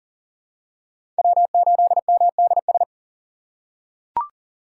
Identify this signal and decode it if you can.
Morse code, heard as a keyed tone, about 40 wpm, 690 Hz. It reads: W8MDS
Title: USA Call Signs - 5 Characters - 6x Character Spacing - 40wpm